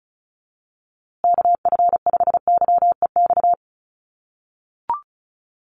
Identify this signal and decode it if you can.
Morse code, heard as a keyed tone, about 34 wpm, 700 Hz. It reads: KF5YEX
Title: USA Call Signs Repeated 3x 35wpm